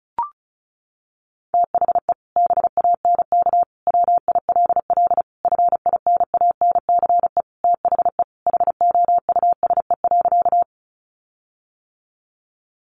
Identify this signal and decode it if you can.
Morse code, heard as a keyed tone, about 34 wpm, 700 Hz. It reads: THE BANK WILL FINANCE THE HOUSE.
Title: Sentences from Top 1500 Words 35wpm